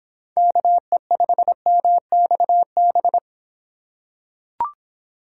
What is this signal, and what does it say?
Morse code, heard as a keyed tone, about 25 wpm, 700 Hz. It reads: KE5MXB